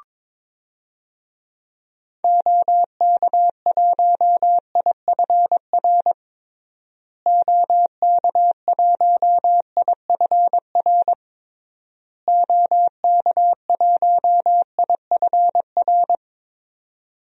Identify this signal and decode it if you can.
Morse code, heard as a keyed tone, about 22 wpm, 700 Hz. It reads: OK1IFR OK1IFR OK1IFR
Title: DXCC-100 Call Signs - 6 Characters - Repeated 3x - 22wpm